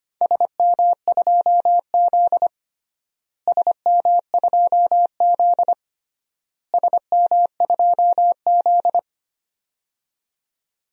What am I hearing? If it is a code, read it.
Morse code, heard as a keyed tone, about 25 wpm, 700 Hz. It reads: SM2Z SM2Z SM2Z
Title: DXCC-100 Call Signs - 4 Characters - Repeated 3x - 25wpm